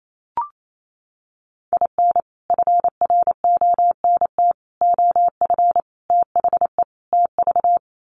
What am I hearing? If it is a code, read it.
Morse code, heard as a keyed tone, about 28 wpm, 700 Hz. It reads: IN FRONT OF THE TV
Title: Sets of 5 Words Encore2 28wpm